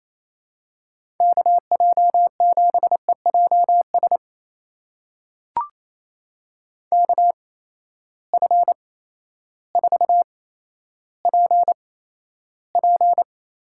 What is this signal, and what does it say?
Morse code, heard as a keyed tone, about 28 wpm, spaced as for 13 wpm, 700 Hz. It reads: KJ7EJS KF4PP
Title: USA Call Signs - 6 Characters - 8x Character Spacing - 28wpm